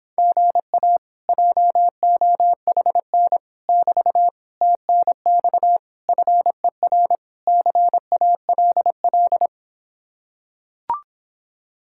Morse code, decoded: GA JOHN <BT> TNX FER CALL

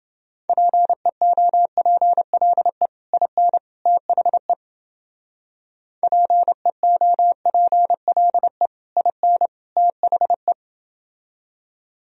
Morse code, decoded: PEOPLE IN THE PEOPLE IN THE

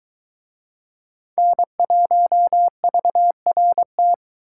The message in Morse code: N1VRT

